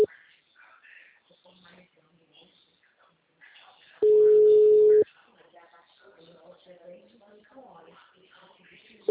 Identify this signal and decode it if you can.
Ringback tone (Europe, 425 Hz), with speech.